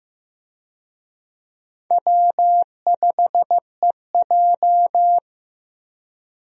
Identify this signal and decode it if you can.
Morse code, heard as a keyed tone, about 16 wpm, 700 Hz. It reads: W5EJ
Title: USA Call Signs - 4 Characters - Repeated 3x - 15wpm